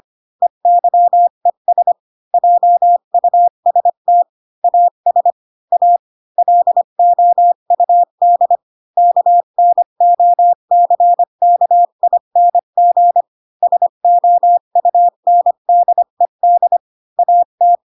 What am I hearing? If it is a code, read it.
Morse code, heard as a keyed tone, about 25 wpm, 700 Hz. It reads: EYES JUST AS A LOUD KNOCKING SOUNDED AT